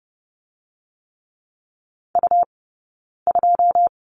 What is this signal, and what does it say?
Morse code, heard as a keyed tone, about 30 wpm, 700 Hz. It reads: U 2
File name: Lesson 17d - Letter u - Two Characters 30wpm